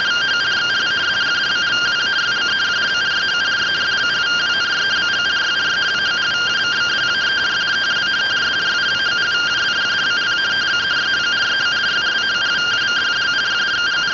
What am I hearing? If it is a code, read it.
Two-tone teleprinter signal, about 170 Hz shift.